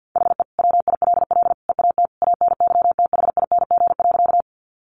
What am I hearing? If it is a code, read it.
Morse code, heard as a keyed tone, about 50 wpm, 700 Hz. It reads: HE WILL EAT ANYTHING.
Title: Sentences from Top 1000 Words 50wpm